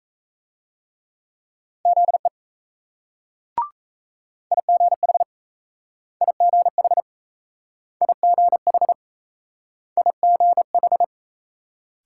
Morse code, heard as a keyed tone, about 38 wpm, 700 Hz. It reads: ZE IGH IGH IGH IGH